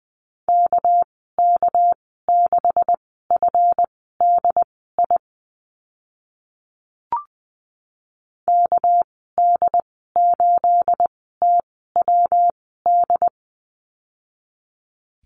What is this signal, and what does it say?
Morse code, heard as a keyed tone, about 20 wpm, spaced as for 14 wpm, 700 Hz. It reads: KK6FDI KD8TWD
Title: 238 Lesson 38g - Letter q - Call Signs 20wpm